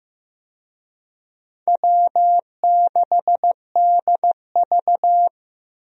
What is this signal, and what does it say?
Morse code, heard as a keyed tone, about 15 wpm, 700 Hz. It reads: W6DV
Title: USA Call Signs - 4 Characters - Repeated 2x - 15wpm